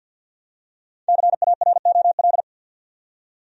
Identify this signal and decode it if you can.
Morse code, heard as a keyed tone, about 52 wpm, 700 Hz. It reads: CAROL
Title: USA Common Names - 50wpm